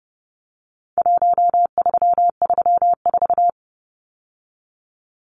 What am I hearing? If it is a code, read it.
Morse code, heard as a keyed tone, about 30 wpm, 700 Hz. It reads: 1334
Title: Numbers - Contest Style - 4-Digit 30wpm